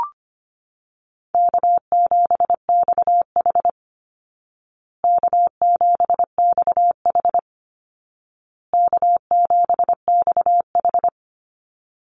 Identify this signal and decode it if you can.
Morse code, heard as a keyed tone, about 25 wpm, 700 Hz. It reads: K7XH K7XH K7XH
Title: Easy Call Signs Repeated 3x - 25wpm